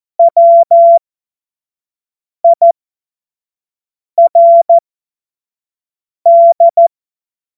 Morse code, heard as a keyed tone, about 14 wpm, spaced as for 5 wpm, 680 Hz. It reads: WIRD